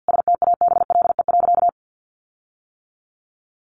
Morse code, heard as a keyed tone, about 50 wpm, 700 Hz. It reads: STABLE.
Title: Sentences from Top 1500 Words 50wpm